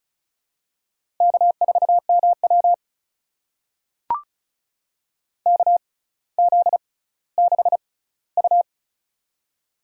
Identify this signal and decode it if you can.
Morse code, heard as a keyed tone, about 36 wpm, spaced as for 18 wpm, 700 Hz. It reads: K4MW KZ6U
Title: USA Call Signs - 4 Characters - 6x Character Spacing - 35wpm